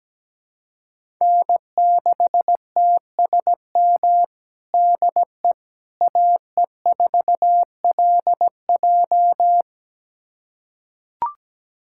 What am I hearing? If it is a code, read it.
Morse code, heard as a keyed tone, about 17 wpm, 700 Hz. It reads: N6TSM DE AE4LJ
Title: USA Callsign DE Callsign - 5 Characters - 17wpm